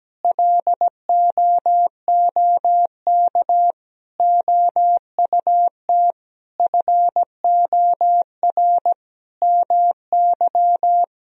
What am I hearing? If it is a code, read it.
Morse code, heard as a keyed tone, about 17 wpm, 700 Hz. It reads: LOOK OUT FOR MY
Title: Sentences from Top 1000 Words - Review Part 4 - 17wpm